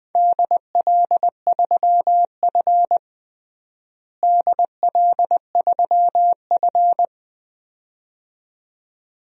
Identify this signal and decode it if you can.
Morse code, heard as a keyed tone, about 20 wpm, 700 Hz. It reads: DL3F DL3F
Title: DXCC-100 Call Signs - 4 Characters - Repeated 2x - 20wpm